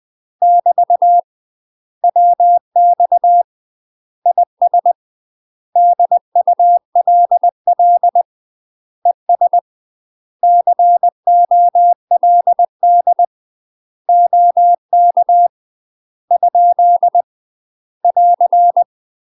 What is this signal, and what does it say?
Morse code, heard as a keyed tone, about 20 wpm, 700 Hz. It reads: <BT> WX IS DULL ES COLD OK ? <AR>
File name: UK MASTER QSOs TRAINING / 20WPM / UK MASTER QSOs 0047